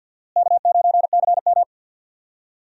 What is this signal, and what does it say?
Morse code, heard as a keyed tone, about 49 wpm, 700 Hz. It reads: K9CM
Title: USA Call Signs - 4 Characters - Repeated 2x - 50wpm